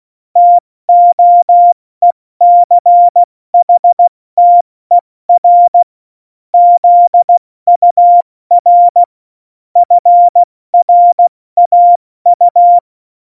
Morse code, heard as a keyed tone, about 16 wpm, spaced as for 14 wpm, 700 Hz. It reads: TOECHTER ZUR FRAU
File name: Morsekurs / Witze / Witz 18. Juni 2019, 70 F80